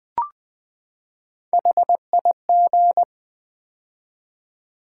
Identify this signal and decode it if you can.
Morse code, heard as a keyed tone, about 20 wpm, 700 Hz. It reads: HIG
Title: Build A Word - 3 Letters At A Time - 6-Letter Words - 20wpm